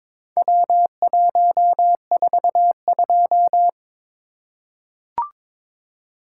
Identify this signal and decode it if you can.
Morse code, heard as a keyed tone, about 22 wpm, 700 Hz. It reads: W142
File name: practice / 22wpm / Four Letter-Number Combinations - 22wpm